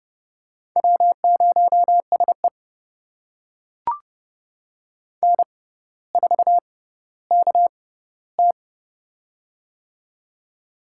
Morse code, heard as a keyed tone, about 29 wpm, spaced as for 15 wpm, 700 Hz. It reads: W0SE N4KT
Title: USA Call Signs - 4 Characters - 6x Character Spacing - 30wpm